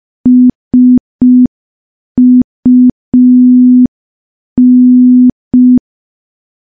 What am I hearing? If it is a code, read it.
Morse code, heard as a keyed tone, about 5 wpm, 250 Hz. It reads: SUN
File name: morse / morse-training-05WPM